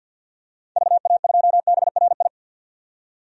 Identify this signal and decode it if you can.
Morse code, heard as a keyed tone, about 51 wpm, 700 Hz. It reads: VA2BRI
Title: DXCC-100 Call Signs - 6 Characters - 50wpm